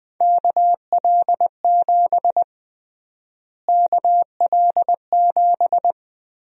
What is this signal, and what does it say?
Morse code, heard as a keyed tone, about 20 wpm, 700 Hz. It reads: KL7 KL7
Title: USA Call Sign Prefixes - Repeated 2x - 20wpm